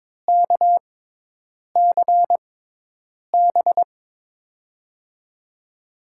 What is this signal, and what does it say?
Morse code, heard as a keyed tone, about 22 wpm, spaced as for 8 wpm, 700 Hz. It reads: KCB
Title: USA Call Signs - 6x Character Spacing - 22wpm